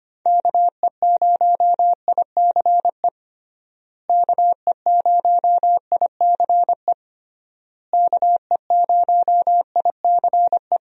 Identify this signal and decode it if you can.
Morse code, heard as a keyed tone, about 25 wpm, 700 Hz. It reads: KE0ICE KE0ICE KE0ICE
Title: USA Call Signs - 6 Characters - Repeated 3x - 25wpm